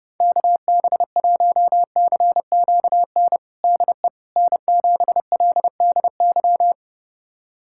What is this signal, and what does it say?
Morse code, heard as a keyed tone, about 30 wpm, 700 Hz. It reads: KB1CQN DE N7LDY
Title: USA Callsign DE Callsign 30wpm